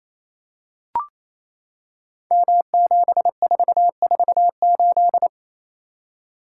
Morse code, heard as a keyed tone, about 29 wpm, 700 Hz. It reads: M7448
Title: Code Groups - 5 Letter:Numbers - 28wpm